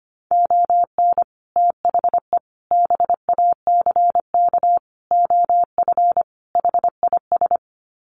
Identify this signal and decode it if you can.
Morse code, heard as a keyed tone, about 25 wpm, 700 Hz. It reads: ON THE BACK OF HIS
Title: Sets of 5 Words Encore 25wpm